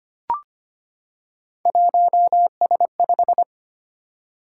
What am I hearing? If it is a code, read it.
Morse code, heard as a keyed tone, about 25 wpm, 700 Hz. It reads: 1S5
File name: Three Letter-Number Combinations - 25wpm